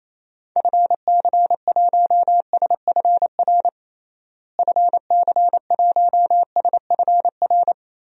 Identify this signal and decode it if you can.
Morse code, heard as a keyed tone, about 28 wpm, 700 Hz. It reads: FC1SFR FC1SFR
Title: DXCC-100 Call Signs - 6 Characters - Repeated 3x - 28wpm